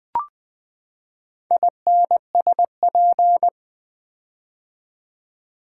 Morse code, heard as a keyed tone, about 20 wpm, 700 Hz. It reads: INSP
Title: Build A Word - 4 Letters At A Time - 11-Letter Words - 20wpm